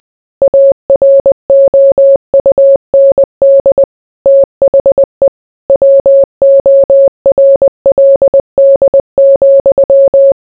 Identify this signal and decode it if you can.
Morse code, heard as a keyed tone, about 20 wpm, 550 Hz. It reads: AROUND THE WORLD,